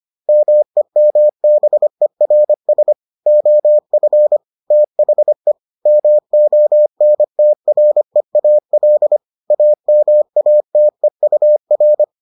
Morse code, heard as a keyed tone, about 25 wpm, 600 Hz. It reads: MEMBERS OF THE MONTREAL AMATEUR